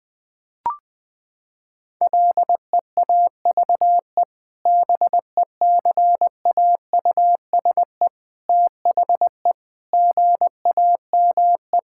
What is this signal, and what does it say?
Morse code, heard as a keyed tone, about 20 wpm, 700 Hz. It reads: LEAVE BECAUSE THE GAME
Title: Sentences with 2-Word Phrases - 20wpm